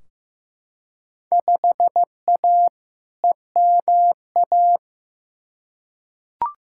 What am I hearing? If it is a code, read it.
Morse code, heard as a keyed tone, about 15 wpm, 700 Hz. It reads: 5A EMA
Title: Field Day Exchanges Encore - 15wpm